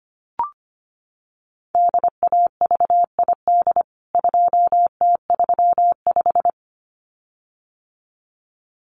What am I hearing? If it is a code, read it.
Morse code, heard as a keyed tone, about 25 wpm, 700 Hz. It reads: DAVID 2T35
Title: CWT Exchange 25wpm